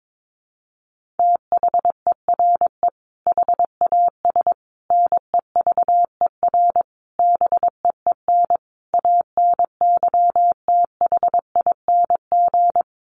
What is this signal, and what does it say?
Morse code, heard as a keyed tone, about 22 wpm, 700 Hz. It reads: THERE HAS NEVER BEEN ANYTHING
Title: Sentences from Top 1500 Words 22wpm